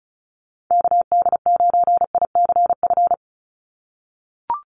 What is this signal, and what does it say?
Morse code, heard as a keyed tone, about 34 wpm, 700 Hz. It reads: KD9ICF
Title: USA Call Signs 35wpm